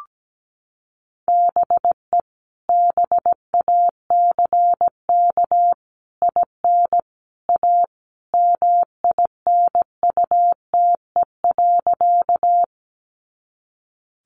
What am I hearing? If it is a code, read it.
Morse code, heard as a keyed tone, about 17 wpm, 700 Hz. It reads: BE BACK IN A MINUTE.
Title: Sentences from Top 500 Words 17wpm